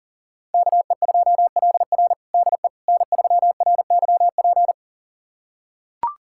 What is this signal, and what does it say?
Morse code, heard as a keyed tone, about 40 wpm, 700 Hz. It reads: KE2LR DE N3RYP
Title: USA Callsign DE Callsign - 5 Characters - 40wpm